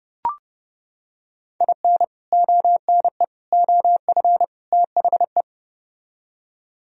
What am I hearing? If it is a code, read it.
Morse code, heard as a keyed tone, about 30 wpm, 700 Hz. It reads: IN ONE OF THE